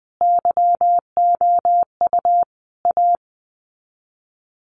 Morse code, heard as a keyed tone, about 20 wpm, 700 Hz. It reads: YOU A
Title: Sets of 5 Words Encore 20wpm